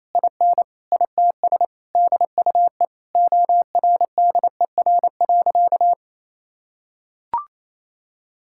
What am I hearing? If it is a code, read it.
Morse code, heard as a keyed tone, about 28 wpm, 700 Hz. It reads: IN ITS DUE ORDER.
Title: Sentences from Top 1000 Words - Review Part 6 - 28wpm